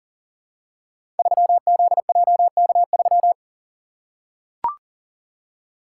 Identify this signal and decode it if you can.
Morse code, heard as a keyed tone, about 39 wpm, 700 Hz. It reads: 3ZJK3
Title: Code Groups - 5 Letter:Numbers - 40wpm